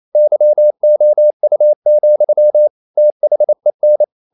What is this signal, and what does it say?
Morse code, heard as a keyed tone, about 28 wpm, 600 Hz. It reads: YOU, THEN
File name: CW / Practice / EnglishQuotations0228